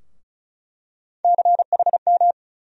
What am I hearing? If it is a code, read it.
Morse code, heard as a keyed tone, about 36 wpm, 700 Hz. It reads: CHM
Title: USA Call Sign Postfixes - 3 Letters 35wpm